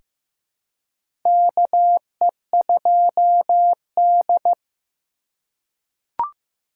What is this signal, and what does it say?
Morse code, heard as a keyed tone, about 15 wpm, 700 Hz. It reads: KE2D
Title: K1USN SST QSO Practice - 15wpm